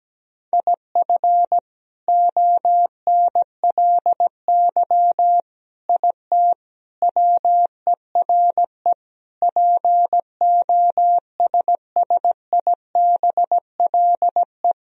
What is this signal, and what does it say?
Morse code, heard as a keyed tone, about 17 wpm, 700 Hz. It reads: IF ONLY IT WERE POSSIBLE
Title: Sentences from Top 500 Words and Abbreviations 17wpm